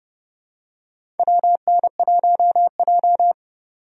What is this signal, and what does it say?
Morse code, heard as a keyed tone, about 30 wpm, 700 Hz. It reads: WN1J
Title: USA Call Signs - 4 Characters - 30wpm